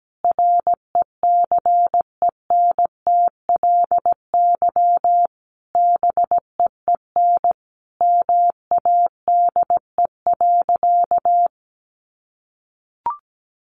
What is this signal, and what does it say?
Morse code, heard as a keyed tone, about 17 wpm, 700 Hz. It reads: RECENTLY BEEN MADE.
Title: Sentences from Top 1200 Words 17wpm